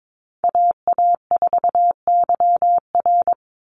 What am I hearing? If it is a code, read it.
Morse code, heard as a keyed tone, about 22 wpm, 700 Hz. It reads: AA4YR
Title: USA Call Signs 22wpm